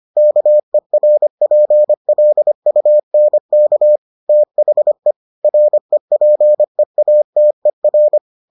Morse code, heard as a keyed tone, about 25 wpm, 600 Hz. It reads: KERPLUNK THE REPEATER